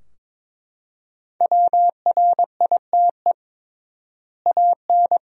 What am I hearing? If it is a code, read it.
Morse code, heard as a keyed tone, about 22 wpm, 700 Hz. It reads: WRITE AN
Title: 3x Word-Spacing - Sentences from Top 300 Words - 22wpm